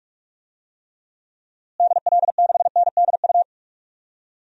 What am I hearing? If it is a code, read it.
Morse code, heard as a keyed tone, about 44 wpm, 700 Hz. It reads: DL6NDU
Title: DXCC-100 Call Signs - 6 Characters - Repeated 2x - 45wpm